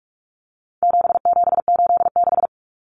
Morse code, heard as a keyed tone, about 44 wpm, 700 Hz. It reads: 7786
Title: Numbers - 4-Digit 45wpm